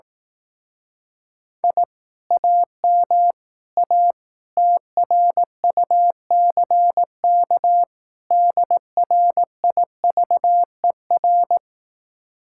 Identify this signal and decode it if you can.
Morse code, heard as a keyed tone, about 18 wpm, 700 Hz. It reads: I AM A TRUCK DRIVER
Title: Speed-Racing Common QSO Phrases Encore - 15wpm